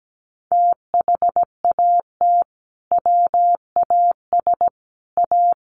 Morse code, heard as a keyed tone, about 17 wpm, 700 Hz. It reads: THAT WAS A
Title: Sentences from Top 800 Words 17wpm